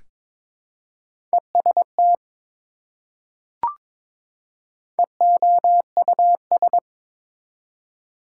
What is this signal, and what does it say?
Morse code, heard as a keyed tone, about 22 wpm, 700 Hz. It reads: EST EOUS
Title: Prefixes and Suffixes 22wpm